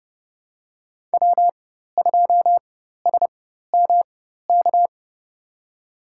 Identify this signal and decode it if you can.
Morse code, heard as a keyed tone, about 32 wpm, spaced as for 14 wpm, 700 Hz. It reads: W2SMK
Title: USA Call Signs - 4x Character Spacing - 30wpm